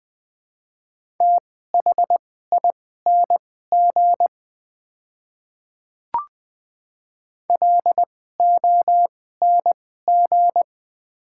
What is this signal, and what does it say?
Morse code, heard as a keyed tone, about 20 wpm, spaced as for 14 wpm, 700 Hz. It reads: THING LONG
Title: 2x Character Spacing - Top 300 Words - 20wpm